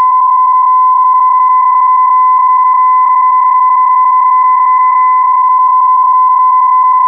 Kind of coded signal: PSK31